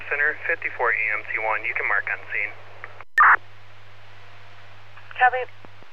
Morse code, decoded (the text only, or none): none